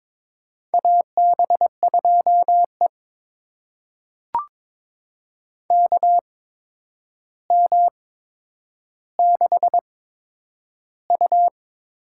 Morse code, heard as a keyed tone, about 22 wpm, spaced as for 9 wpm, 700 Hz. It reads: AB2E KM6U
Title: USA Call Signs - 8x Character Spacing - 22wpm